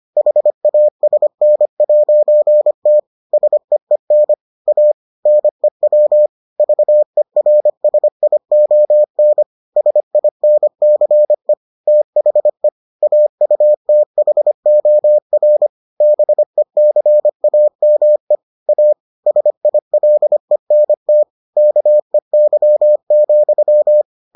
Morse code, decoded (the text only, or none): HASN'T SEEN A NEW VERSION SINCE THE AUTHOR BECAME A SILENT KEY,